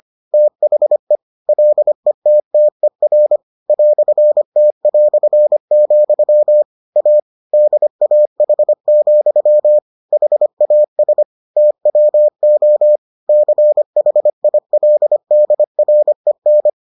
Morse code, heard as a keyed tone, about 25 wpm, 600 Hz. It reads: THE LETTER "T", A DAH, HAS TWO CHILDREN